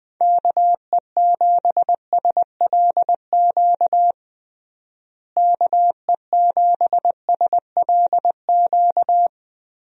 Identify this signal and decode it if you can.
Morse code, heard as a keyed tone, about 20 wpm, 700 Hz. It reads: KE7SLQ KE7SLQ
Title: USA Call Signs - 6 Characters - Repeated 3x - 20wpm